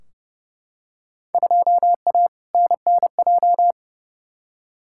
Morse code, heard as a keyed tone, about 30 wpm, 700 Hz. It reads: 2A NNJ
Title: Field Day Exchanges - 30wpm